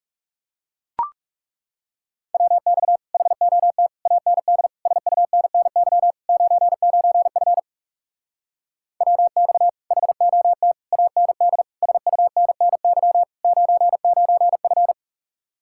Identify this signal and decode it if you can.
Morse code, heard as a keyed tone, about 37 wpm, 700 Hz. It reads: WX HOT AND SUNNY 99F WX HOT AND SUNNY 99F
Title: Speed-Racing Common QSO Phrases Encore - 30wpm